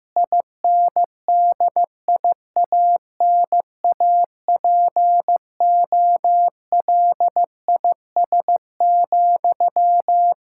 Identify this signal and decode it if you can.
Morse code, heard as a keyed tone, about 15 wpm, 700 Hz. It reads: INDIANAPOLIS,
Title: Practice QSO - Exchange 1 - Beginning - 15wpm